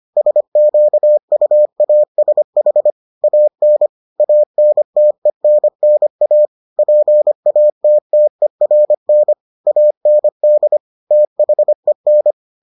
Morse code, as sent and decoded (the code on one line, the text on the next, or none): ... --.- ..- .- ... .... / .- -. / .- -. - . -. -. .- / .--. .- - - . .-. -. / .- -. -.. / - .... -.
SQUASH AN ANTENNA PATTERN AND THEN